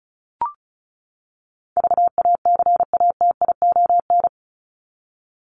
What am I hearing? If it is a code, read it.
Morse code, heard as a keyed tone, about 34 wpm, 700 Hz. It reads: VACATION
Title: Top 1501-1750 Words 35wpm